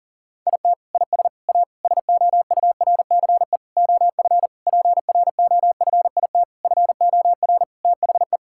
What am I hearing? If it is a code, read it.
Morse code, heard as a keyed tone, about 40 wpm, 700 Hz. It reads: IT IS A SOURCE OF PROFIT FOR THE
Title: Sentences from Top 1000 Words - Review Part 5 - 40wpm